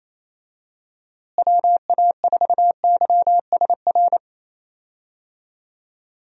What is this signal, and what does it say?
Morse code, heard as a keyed tone, about 28 wpm, 700 Hz. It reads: WA4YSR
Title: USA Call Signs - 6 Characters - Repeated 3x - 28wpm